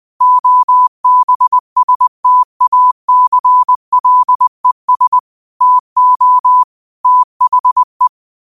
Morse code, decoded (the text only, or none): OBSTACLES TO THE